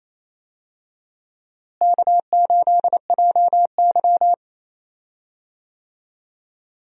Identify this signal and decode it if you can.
Morse code, heard as a keyed tone, about 28 wpm, 700 Hz. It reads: K8JY
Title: USA Call Signs - 4 Characters - Repeated 2x - 28wpm